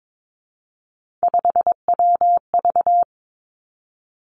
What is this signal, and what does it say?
Morse code, heard as a keyed tone, about 22 wpm, 700 Hz. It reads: 5WV
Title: Three Challenging Letter-Number Combinations 22wpm